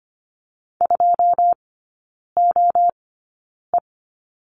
Morse code, heard as a keyed tone, about 25 wpm, spaced as for 9 wpm, 700 Hz. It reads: 2OE
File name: Lesson 14e - Number 2 - Three Characters 25wpm